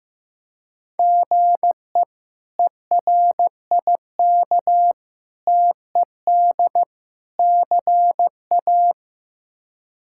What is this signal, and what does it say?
Morse code, heard as a keyed tone, about 15 wpm, 700 Hz. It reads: GE ERIK TED CA
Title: K1USN SST QSO Practice - 15wpm